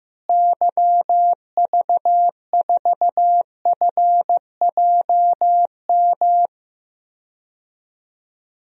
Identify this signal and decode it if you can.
Morse code, heard as a keyed tone, about 15 wpm, 700 Hz. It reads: YV4FJM
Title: DXCC-100 Call Signs - 6 Characters - Repeated 2x - 15wpm